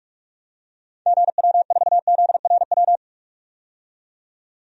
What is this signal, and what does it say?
Morse code, heard as a keyed tone, about 45 wpm, 700 Hz. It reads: GW4ZRW
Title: DXCC-100 Call Signs - 6 Characters - 45wpm